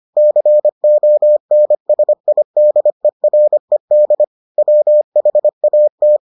Morse code, heard as a keyed tone, about 25 wpm, 600 Hz. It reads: CONSIDERED WHAT